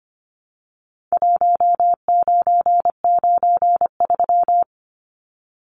Morse code, heard as a keyed tone, about 25 wpm, 700 Hz. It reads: 1993
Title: Numbers - Contest Style - 4-Digit 25wpm